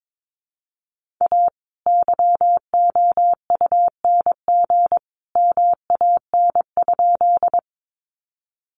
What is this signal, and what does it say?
Morse code, heard as a keyed tone, about 22 wpm, 700 Hz. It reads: A YOUNG MAN?
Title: Sentences from Top 400 Words 22wpm